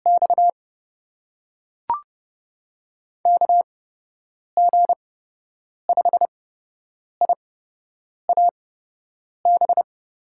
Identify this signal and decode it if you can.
Morse code, heard as a keyed tone, about 30 wpm, spaced as for 8 wpm, 700 Hz. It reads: X KG5IAB
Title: USA Call Signs - 6 Characters - 8x Character Spacing - 30wpm